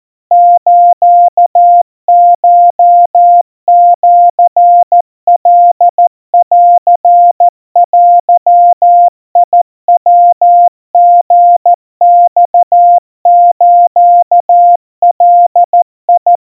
Morse code (Japanese Wabun code), heard as a keyed tone, about 14 wpm, 700 Hz. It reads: スコシカンデヤリマスガ
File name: kawa / Morse sample / kamyu 50 cpm 700 Hz Crispy